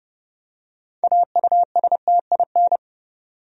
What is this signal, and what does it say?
Morse code, heard as a keyed tone, about 29 wpm, 700 Hz. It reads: AUSTIN
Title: USA Common Names - 30wpm